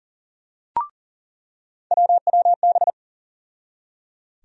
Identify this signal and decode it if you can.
Morse code, heard as a keyed tone, about 39 wpm, 700 Hz. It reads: WWB
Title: USA Call Sign Postfixes - 3 Letters 40wpm